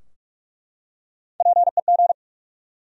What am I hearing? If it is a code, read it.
Morse code, heard as a keyed tone, about 42 wpm, 700 Hz. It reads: PEG
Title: USA Common Names - Max 5 Letters - 45wpm